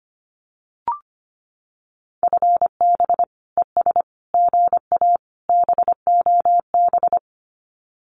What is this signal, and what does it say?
Morse code, heard as a keyed tone, about 25 wpm, 700 Hz. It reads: FB ES GA BOB
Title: Common QSO Phrases Encore 25wpm